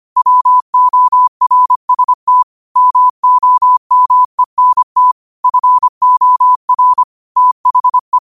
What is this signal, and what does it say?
Morse code, heard as a keyed tone, about 25 wpm, 1000 Hz. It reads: WORST MOMENT FOR THE
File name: qotdfiles / 25wpm